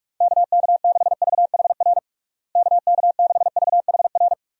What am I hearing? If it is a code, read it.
Morse code, heard as a keyed tone, about 46 wpm, 700 Hz. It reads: KK6VHR KK6VHR
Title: USA Call Signs - 6 Characters - Repeated 3x - 45wpm